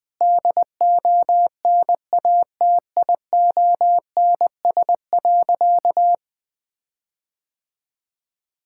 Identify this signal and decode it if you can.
Morse code, heard as a keyed tone, about 20 wpm, 700 Hz. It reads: DONATIONS.
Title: Sentences with 9-Word Phrases - 20wpm